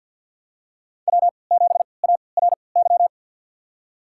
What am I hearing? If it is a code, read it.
Morse code, heard as a keyed tone, about 53 wpm, spaced as for 29 wpm, 700 Hz. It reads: W7ARY
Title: USA Call Signs - 5 Characters - 3x Character Spacing - 50wpm